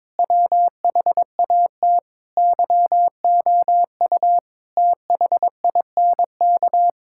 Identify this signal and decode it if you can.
Morse code, heard as a keyed tone, about 22 wpm, 700 Hz. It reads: WHAT YOU THINK
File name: Sentences with 7-Word Phrases - 22wpm